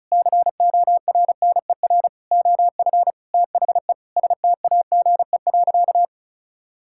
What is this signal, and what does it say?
Morse code, heard as a keyed tone, about 36 wpm, 700 Hz. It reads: CORNER OF THE STAGE.
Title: Sentences from Top 1600 Words 35wpm